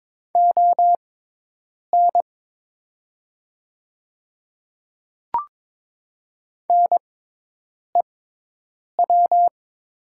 Morse code, heard as a keyed tone, about 22 wpm, spaced as for 8 wpm, 700 Hz. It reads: ON NEW
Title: 6x Character Spacing - Top 200 Words - 22wpm